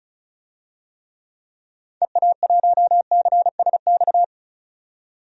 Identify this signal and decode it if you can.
Morse code, heard as a keyed tone, about 36 wpm, 700 Hz. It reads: EA1CSX
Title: DXCC-100 Call Signs - 6 Characters - 35wpm